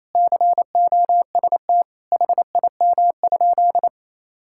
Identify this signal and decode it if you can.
Morse code, heard as a keyed tone, about 28 wpm, 700 Hz. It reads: COST HIM?
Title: Sentences from Top 1000 Words - Review Part 5 - 28wpm